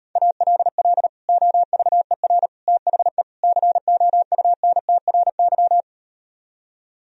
Morse code, heard as a keyed tone, about 38 wpm, 700 Hz. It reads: ALL OVER THE COUNTRY